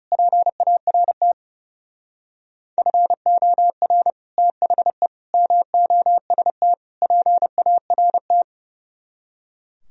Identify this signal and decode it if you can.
Morse code, heard as a keyed tone, about 36 wpm, 700 Hz. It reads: PART FOR THE MOST PART
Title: Speed-Racing - Sets of 4 Words 30wpm